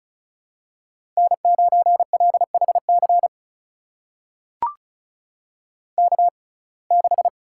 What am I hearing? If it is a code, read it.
Morse code, heard as a keyed tone, about 36 wpm, spaced as for 25 wpm, 700 Hz. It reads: N9LHC K6
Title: USA Call Signs - 6x Character Spacing - 35wpm